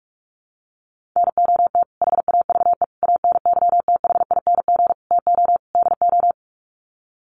Morse code, heard as a keyed tone, about 44 wpm, 700 Hz. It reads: NOT HAVE ANYTHING TO DO
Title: Sets of 5 Words Encore 45wpm